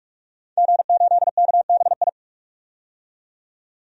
Morse code, heard as a keyed tone, about 44 wpm, 690 Hz. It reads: G8KBI
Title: DXCC-100 Call Signs - 5 Characters - 45wpm